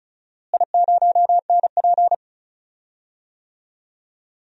Morse code, heard as a keyed tone, about 36 wpm, 700 Hz. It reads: I0NP
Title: DXCC-100 Call Signs - 4 Characters - Repeated 3x - 35wpm